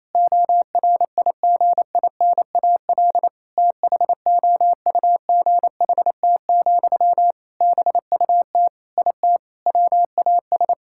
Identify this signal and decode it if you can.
Morse code, heard as a keyed tone, about 28 wpm, 700 Hz. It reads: ORIGINAL THOUGHT, BUT IT WAS
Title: Sentences from Top 1000 Words - Review Part 6 - 28wpm